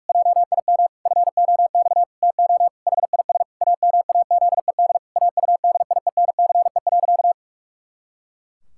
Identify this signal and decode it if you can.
Morse code, heard as a keyed tone, about 41 wpm, 700 Hz. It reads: JIM FOX TO HIS AMAZED AUDIENCE.
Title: Pangram Challenge v2 - 45wpm